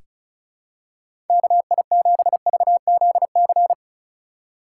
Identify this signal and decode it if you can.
Morse code, heard as a keyed tone, about 34 wpm, 700 Hz. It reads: KI7VZC
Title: USA Call Signs - 6 Characters - 4x Character Spacing - 35wpm